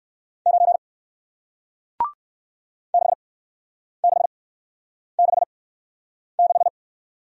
Morse code, heard as a keyed tone, about 58 wpm, 700 Hz. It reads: 9 6 6 6 6